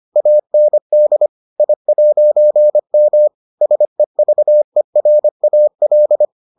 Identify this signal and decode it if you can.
Morse code, heard as a keyed tone, about 25 wpm, 600 Hz. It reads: AND I'M SEVERAL